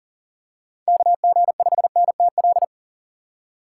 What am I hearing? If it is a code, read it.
Morse code, heard as a keyed tone, about 40 wpm, 700 Hz. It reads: KG5NTL
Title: USA Call Signs - 6 Characters - 2x Character Spacing - 40wpm